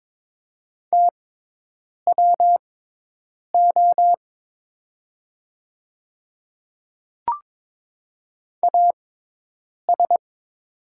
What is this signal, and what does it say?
Morse code, heard as a keyed tone, about 22 wpm, spaced as for 8 wpm, 700 Hz. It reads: TWO AS
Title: 6x Character Spacing - Top 100 Words - 22wpm